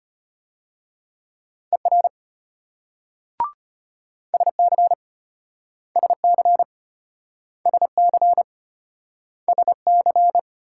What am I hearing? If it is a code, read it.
Morse code, heard as a keyed tone, about 34 wpm, 700 Hz. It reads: ER SC SC SC SC